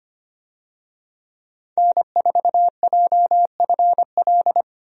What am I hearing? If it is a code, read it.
Morse code, heard as a keyed tone, about 25 wpm, 700 Hz. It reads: N4JFL
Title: USA Call Signs - 5 Characters - 25wpm